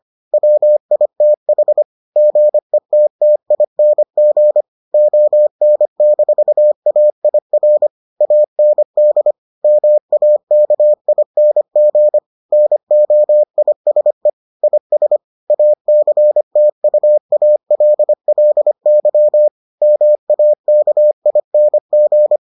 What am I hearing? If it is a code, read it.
Morse code, heard as a keyed tone, about 25 wpm, 600 Hz. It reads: WITH GETTING ON-AIR AND MAKING NOISE IS ACTUALLY MAKING